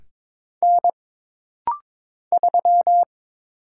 Morse code, heard as a keyed tone, about 22 wpm, 700 Hz. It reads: N 3
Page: This practice set is composed of Letters, Numbers, common punctuations, and common prosigns.